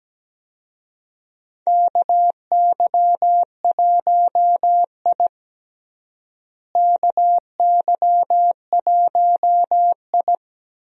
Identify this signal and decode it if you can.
Morse code, heard as a keyed tone, about 17 wpm, 700 Hz. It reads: KY1I KY1I
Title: USA Call Signs - 4 Characters - Repeated 2x - 17wpm